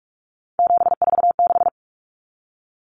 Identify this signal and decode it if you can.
Morse code, heard as a keyed tone, about 44 wpm, 700 Hz. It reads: 746
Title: Common Cut Numbers - 3-Digit 45wpm